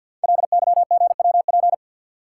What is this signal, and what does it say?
Morse code, heard as a keyed tone, about 53 wpm, 700 Hz. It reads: LYGWP
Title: Code Groups - 5 Letter:Numbers - 50wpm